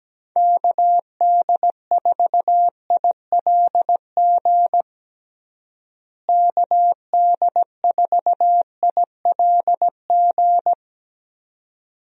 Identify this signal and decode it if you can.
Morse code, heard as a keyed tone, about 17 wpm, 700 Hz. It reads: KD4ILG KD4ILG
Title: USA Call Signs - 6 Characters - Repeated 2x - 17wpm